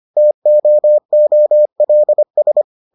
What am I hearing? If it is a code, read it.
Morse code, heard as a keyed tone, about 25 wpm, 600 Hz. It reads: TOOLS